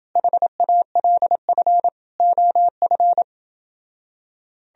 Morse code, heard as a keyed tone, about 28 wpm, 700 Hz. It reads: HALF OF